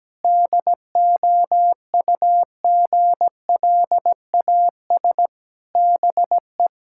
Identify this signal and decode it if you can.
Morse code, heard as a keyed tone, about 17 wpm, 700 Hz. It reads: DOUGLAS BE
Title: Pangram Challenge v2 - 17wpm